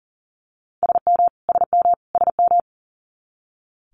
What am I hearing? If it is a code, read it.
Morse code, heard as a keyed tone, about 41 wpm, 700 Hz. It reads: SM SM SM
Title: Multi-letter Phonemes Repeated 3x 40wpm